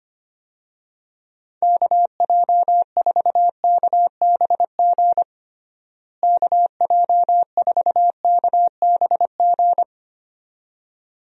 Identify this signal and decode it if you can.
Morse code, heard as a keyed tone, about 25 wpm, 700 Hz. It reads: KJ4KBG KJ4KBG
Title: USA Call Signs - 6 Characters - Repeated 2x - 25wpm